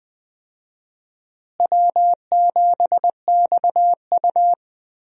Morse code, heard as a keyed tone, about 20 wpm, 700 Hz. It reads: W7XU
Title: USA Call Signs - 4 Characters - Repeated 2x - 20wpm